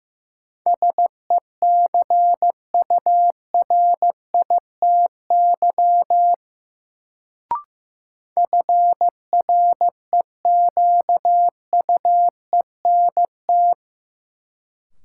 Morse code, heard as a keyed tone, about 15 wpm, 700 Hz. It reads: SECURITY FREQUENT
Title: Sentences from Top 1400 Words - New Word Drill Alt - 15wpm